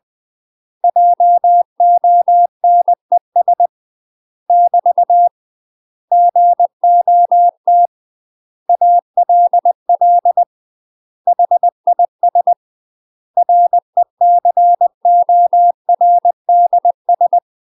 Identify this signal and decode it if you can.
Morse code, heard as a keyed tone, about 20 wpm, 700 Hz. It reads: JONES <BT> GOT ALL HIS RECORDS
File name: UK MASTER QSOs TRAINING / 20WPM / UK MASTER QSOs 0060